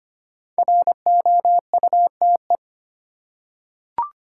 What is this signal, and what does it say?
Morse code, heard as a keyed tone, about 25 wpm, 700 Hz. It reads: ROUTE
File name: 5 Letter Words - First 3 Letters Given - 25wpm